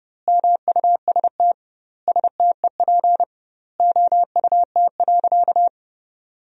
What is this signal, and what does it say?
Morse code, heard as a keyed tone, about 30 wpm, 700 Hz. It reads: MUST STEP OUT.
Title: 2x Word-Spacing - Sentences from Top 400 Words - 30wpm